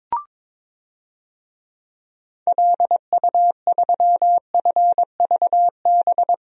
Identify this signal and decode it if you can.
Morse code, heard as a keyed tone, about 22 wpm, 700 Hz. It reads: LU3FVB
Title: DXCC-100 Call Signs - 6 Characters - Repeated 3x - 22wpm